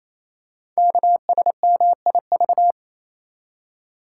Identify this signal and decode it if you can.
Morse code, heard as a keyed tone, about 27 wpm, 700 Hz. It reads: KSMIV
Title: Code Groups - 5 Letters - 28wpm